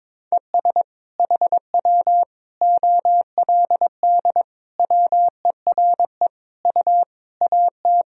Morse code, heard as a keyed tone, about 22 wpm, 700 Hz. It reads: ES HW OLD WERE U AT
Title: Sentences from Top 200 Words and Abbreviations 22wpm